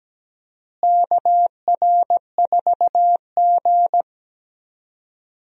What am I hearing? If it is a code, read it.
Morse code, heard as a keyed tone, about 17 wpm, 700 Hz. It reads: KR4G
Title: USA Call Signs - 4 Characters - Repeated 3x - 17wpm